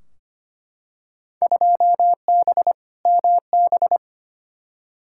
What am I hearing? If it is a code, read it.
Morse code, heard as a keyed tone, about 25 wpm, 700 Hz. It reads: 2B MB
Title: Field Day Exchanges Encore - 25wpm